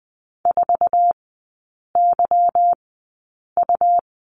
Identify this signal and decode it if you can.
Morse code, heard as a keyed tone, about 20 wpm, spaced as for 8 wpm, 700 Hz. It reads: 4YU
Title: 184 Lesson 30e - Letter b - Three Characters 20wpm